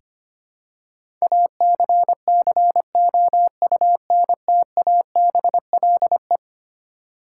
Morse code, heard as a keyed tone, about 25 wpm, 700 Hz. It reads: ACCOUNTABLE
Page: morse-code-ninja-practice-25wpm